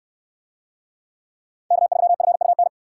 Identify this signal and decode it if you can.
Morse code, heard as a keyed tone, about 68 wpm, 690 Hz. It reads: K3FUN